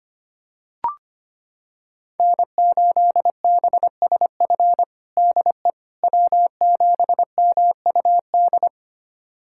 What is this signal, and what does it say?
Morse code, heard as a keyed tone, about 25 wpm, 700 Hz. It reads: N8BSF DE W7MUD
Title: USA Callsign DE Callsign - 5 Characters - 25wpm